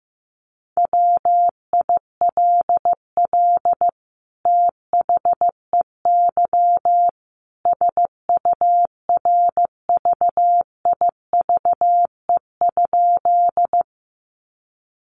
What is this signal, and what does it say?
Morse code, heard as a keyed tone, about 15 wpm, 700 Hz. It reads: WILL THEY SURVIVE?
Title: Sentences from Top 1300 Words 15wpm